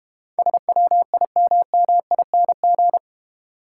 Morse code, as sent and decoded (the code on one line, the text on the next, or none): ... .-- .. -- -- .. -. --.
SWIMMING